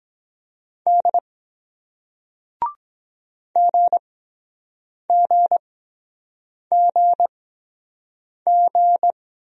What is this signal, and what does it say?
Morse code, heard as a keyed tone, about 23 wpm, spaced as for 7 wpm, 700 Hz. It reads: D GGGG